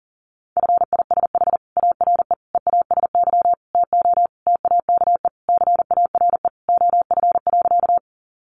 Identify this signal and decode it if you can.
Morse code, heard as a keyed tone, about 40 wpm, 700 Hz. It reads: FISH ARE EASY TO TAKE CARE OF.
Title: Additional Easy Phrases 40wpm